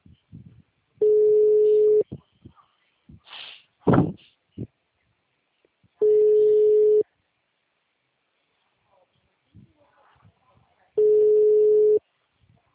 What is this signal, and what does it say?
Ringback tone (Europe, 425 Hz), with speech.